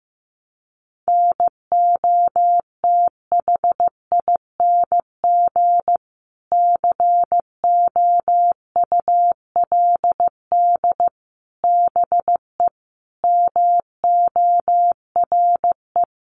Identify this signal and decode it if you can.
Morse code, heard as a keyed tone, about 15 wpm, 700 Hz. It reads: NOTHING COULD BE MORE
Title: Sentences from Top 900 Words 15wpm